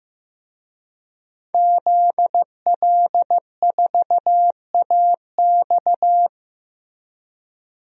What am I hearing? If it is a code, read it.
Morse code, heard as a keyed tone, about 15 wpm, 700 Hz. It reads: ZL4AX
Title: DXCC-100 Call Signs - 5 Characters - Repeated 3x - 15wpm